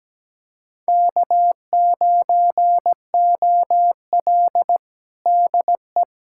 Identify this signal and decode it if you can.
Morse code, heard as a keyed tone, about 17 wpm, 700 Hz. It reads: K9OL DE
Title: USA Callsign DE Callsign - 4 Characters - 17wpm